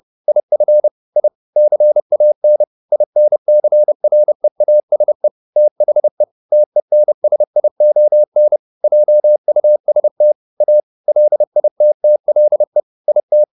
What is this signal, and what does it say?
Morse code, heard as a keyed tone, about 30 wpm, 600 Hz. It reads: IF I CAN INCREASE THE TENSION JUST A LITTLE IT
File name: the old man and the sea / the old man and the sea 1 30wpm